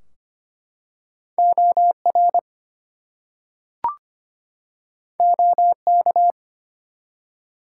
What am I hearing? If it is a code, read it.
Morse code, heard as a keyed tone, about 25 wpm, 700 Hz. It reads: OR OK
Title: Two Letter Words - 25wpm